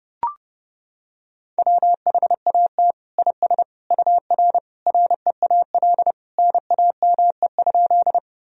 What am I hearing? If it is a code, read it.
Morse code, heard as a keyed tone, about 32 wpm, 700 Hz. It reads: WHAT IS UR REAL NAME?
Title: Sentences from Top 300 Words and Abbreviations 30wpm